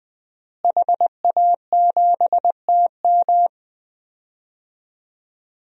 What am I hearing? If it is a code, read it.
Morse code, heard as a keyed tone, about 20 wpm, 700 Hz. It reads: HA7TM
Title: DXCC-100 Call Signs - 5 Characters - Repeated 2x - 20wpm